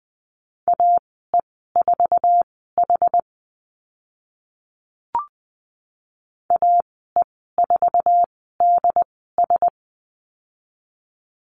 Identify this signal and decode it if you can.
Morse code, heard as a keyed tone, about 20 wpm, spaced as for 14 wpm, 700 Hz. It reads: AE4H AE4DS
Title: 076 Lesson 13g - Letter L - Call Signs 20wpm